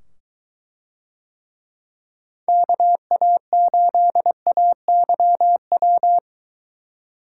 Morse code, heard as a keyed tone, about 22 wpm, 700 Hz. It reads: KA8AYW